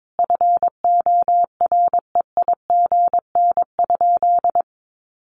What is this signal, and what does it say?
Morse code, heard as a keyed tone, about 22 wpm, 700 Hz. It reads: FOREIGN?
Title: Sentences from Top 1400 Words 22wpm